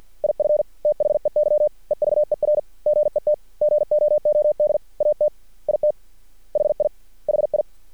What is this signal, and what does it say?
Morse code, heard as a keyed tone, about 50 wpm, 600 Hz. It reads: IF THEY EVER GET GOOD AT IT HI HI
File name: archived / CRSnet